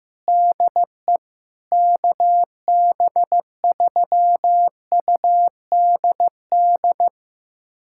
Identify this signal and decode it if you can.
Morse code, heard as a keyed tone, about 15 wpm, 700 Hz. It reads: DE KB3UDD
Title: USA Callsign DE Callsign - 6 Characters - 15wpm